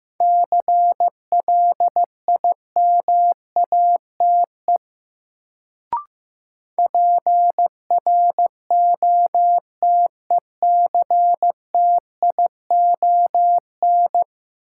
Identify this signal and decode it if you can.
Morse code, heard as a keyed tone, about 15 wpm, 700 Hz. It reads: CLIMATE PROTECTION
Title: Sentences from Top 1600 Words - New Word Drill - 15wpm